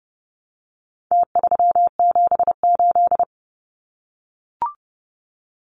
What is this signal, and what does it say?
Morse code, heard as a keyed tone, about 30 wpm, 700 Hz. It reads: T378
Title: Numbers - Contest Style - 4-Digit 30wpm